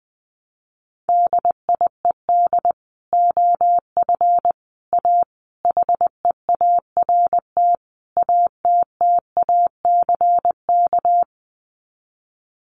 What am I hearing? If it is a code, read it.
Morse code, heard as a keyed tone, about 20 wpm, 700 Hz. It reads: DIED OF A HEART ATTACK
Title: Sets of 5 Words Encore2 20wpm